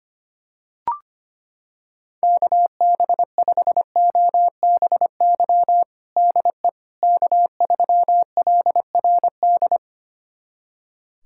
Morse code, heard as a keyed tone, about 25 wpm, 700 Hz. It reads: KB5OBY DE K3LRD
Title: USA Callsign DE Callsign 25wpm